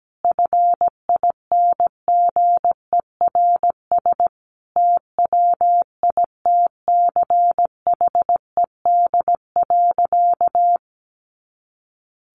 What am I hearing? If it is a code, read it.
Morse code, heard as a keyed tone, about 17 wpm, 700 Hz. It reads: FINGERS TWITCHED.
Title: A Sound Of Thunder 17wpm